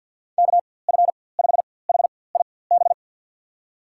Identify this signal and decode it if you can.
Morse code, heard as a keyed tone, about 51 wpm, spaced as for 23 wpm, 700 Hz. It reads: KF5HIB
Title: USA Call Signs - 6 Characters - 4x Character Spacing - 50wpm